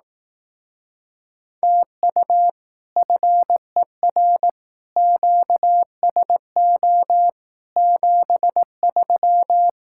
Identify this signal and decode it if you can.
Morse code, heard as a keyed tone, about 18 wpm, 700 Hz. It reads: TU FER QSO 73
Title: Speed-Racing Common QSO Phrases Encore - 15wpm